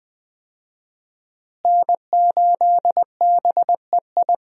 Morse code, heard as a keyed tone, about 21 wpm, 700 Hz. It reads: N8BEI
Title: USA Call Signs - 5 Characters - Repeated 3x - 20wpm